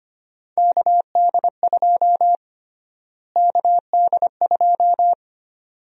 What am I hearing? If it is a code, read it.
Morse code, heard as a keyed tone, about 25 wpm, 700 Hz. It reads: KD2 KD2
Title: USA Call Sign Prefixes - Repeated 2x - 25wpm